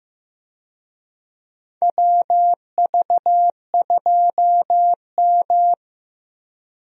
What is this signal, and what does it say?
Morse code, heard as a keyed tone, about 15 wpm, 700 Hz. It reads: WV2M
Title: USA Call Signs - 4 Characters - Repeated 2x - 15wpm